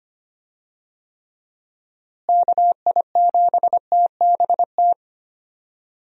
Morse code, heard as a keyed tone, about 25 wpm, 700 Hz. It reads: KI7TBT
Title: USA Call Signs - 6 Characters - Repeated 3x - 25wpm